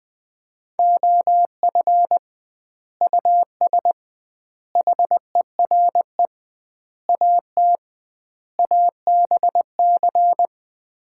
Morse code, heard as a keyed tone, about 21 wpm, 700 Hz. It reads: OF US HERE AT ABC
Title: 2x Word-Spacing - Sets of 5 Words - 20wpm